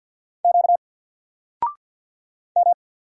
Morse code, heard as a keyed tone, about 48 wpm, 700 Hz. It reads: Q M
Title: Single Character (Letter:Number:Punctuation:Prosign) Alt 50wpm